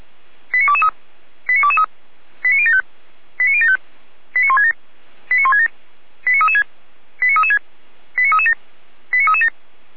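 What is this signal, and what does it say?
Five-tone selective call (ZVEI), heard as quick runs of five tones, tones of about 70 ms. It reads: 89223 89223 89075 89075 89167 89167 89336 89336 89337 89337
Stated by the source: ZVEI five-tone call received by SDR, from the feuerwehr-orlamuende.org fire brigade audio archive.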